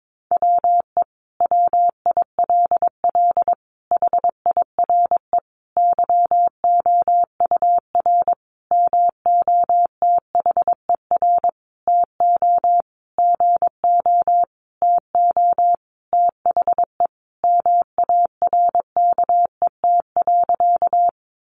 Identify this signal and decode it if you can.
Morse code, heard as a keyed tone, about 22 wpm, 700 Hz. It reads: WE WILL HIRE YOUR MOTHER TO GO TO THE MARKET.
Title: Sentences from Top 1400 Words 22wpm